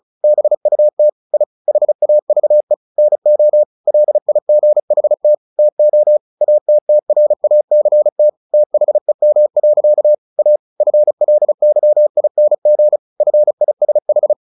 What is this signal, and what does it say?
Morse code, heard as a keyed tone, about 34 wpm, 600 Hz. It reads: BUT I HAVE NO LIGHT TO ATTRACT THEM. A FLYING FISH
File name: the old man and the sea / the old man and the sea 1 35wpm